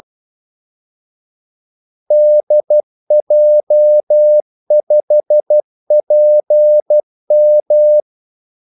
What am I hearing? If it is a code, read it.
Morse code, heard as a keyed tone, about 12 wpm, 600 Hz. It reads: DJ5PM
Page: 2 ekstra ordmellemrum mellem kaldesignalerne